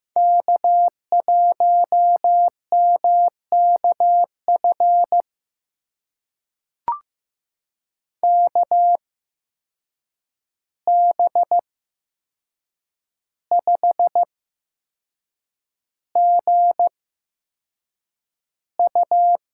Morse code, heard as a keyed tone, about 15 wpm, spaced as for 6 wpm, 700 Hz. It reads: K1MKF KB5GU
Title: USA Call Signs - 5 Characters - 8x Character Spacing - 15wpm